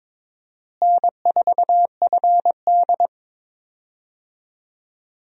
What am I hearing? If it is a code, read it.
Morse code, heard as a keyed tone, about 22 wpm, 700 Hz. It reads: N4FD
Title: USA Call Signs - 4 Characters - Repeated 2x - 22wpm